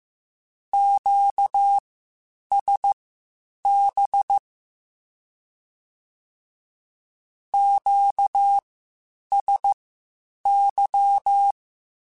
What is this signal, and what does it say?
Morse code, heard as a keyed tone, about 15 wpm, spaced as for 8 wpm, 800 Hz. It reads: QSB QSY
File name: Q-Calls Spaced